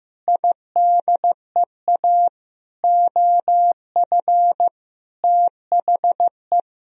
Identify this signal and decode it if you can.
Morse code, heard as a keyed tone, about 15 wpm, 700 Hz. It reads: IDEA OF THE
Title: Sentences from Top 1000 Words - Review Part 3 - 15wpm